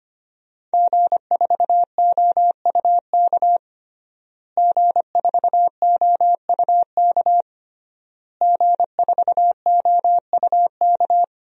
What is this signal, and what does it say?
Morse code, heard as a keyed tone, about 25 wpm, 700 Hz. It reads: G4OUK G4OUK G4OUK
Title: DXCC-100 Call Signs - 5 Characters - Repeated 3x - 25wpm